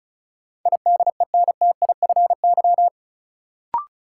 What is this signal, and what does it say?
Morse code, heard as a keyed tone, about 34 wpm, 700 Hz. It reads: IDENTIFY
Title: Sentences from Top 900 Words - New Word Drill - 35wpm